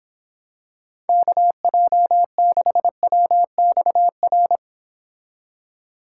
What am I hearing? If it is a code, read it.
Morse code, heard as a keyed tone, about 26 wpm, 700 Hz. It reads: KJ6WXR